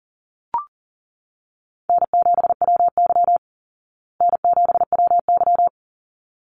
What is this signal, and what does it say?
Morse code, heard as a keyed tone, about 40 wpm, 700 Hz. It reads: N7WY N7WY
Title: Call Signs Repeated 3x - 40wpm